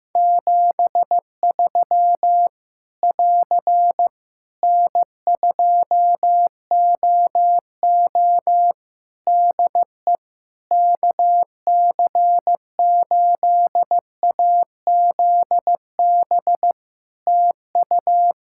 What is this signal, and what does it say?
Morse code, heard as a keyed tone, about 15 wpm, 700 Hz. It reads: 73 <AR> N2OO DE KC8AZB TU
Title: Practice QSO - Exchange 4 - Ending - 15wpm